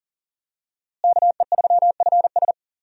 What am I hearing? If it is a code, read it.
Morse code, heard as a keyed tone, about 40 wpm, 700 Hz. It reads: KE3FS
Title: USA Call Signs - 5 Characters - Repeated 2x - 40wpm